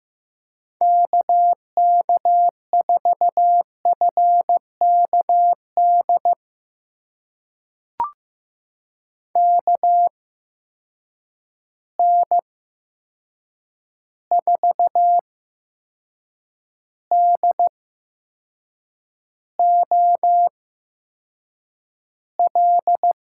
Morse code, heard as a keyed tone, about 15 wpm, spaced as for 6 wpm, 700 Hz. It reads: KK4FKD KN4DOL
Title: USA Call Signs - 6 Characters - 8x Character Spacing - 15wpm